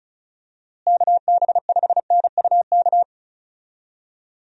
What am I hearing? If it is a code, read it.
Morse code, heard as a keyed tone, about 34 wpm, 700 Hz. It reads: KB5NUK
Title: USA Call Signs - 6 Characters - Repeated 2x - 35wpm